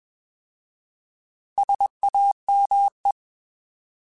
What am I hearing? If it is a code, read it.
Morse code, heard as a keyed tone, about 21 wpm, 800 Hz. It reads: SAME
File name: Alphabetical / Alphabetical mp3s / Ss @20